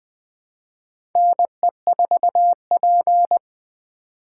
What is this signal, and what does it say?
Morse code, heard as a keyed tone, about 20 wpm, 700 Hz. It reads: NE4P
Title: USA Call Signs - 4x Character Spacing - 20wpm